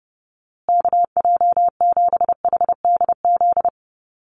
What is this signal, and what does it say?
Morse code, heard as a keyed tone, about 30 wpm, 700 Hz. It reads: KJ7HDZ
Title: USA Call Signs Repeated 3x 30wpm